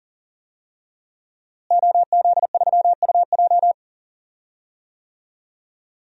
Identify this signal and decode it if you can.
Morse code, heard as a keyed tone, about 37 wpm, 700 Hz. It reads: OZ3UJ
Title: DXCC-100 Call Signs - 5 Characters - Repeated 2x - 40wpm